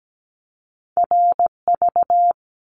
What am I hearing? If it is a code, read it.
Morse code, heard as a keyed tone, about 17 wpm, 700 Hz. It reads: RV
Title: Top Two Letter Combinations in Words 17wpm